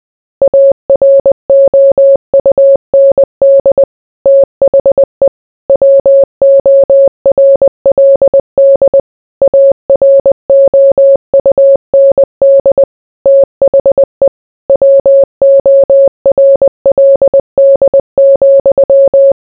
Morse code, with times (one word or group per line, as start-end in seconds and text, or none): 0.4-3.8 AROUND
4.3-5.3 THE
5.7-9.0 WORLD
9.4-12.8 AROUND
13.3-14.3 THE
14.7-19.3 WORLD,